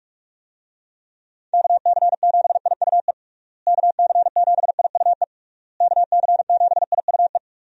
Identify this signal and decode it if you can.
Morse code, heard as a keyed tone, about 46 wpm, 700 Hz. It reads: KC7IUE KC7IUE KC7IUE
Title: USA Call Signs - 6 Characters - Repeated 3x - 45wpm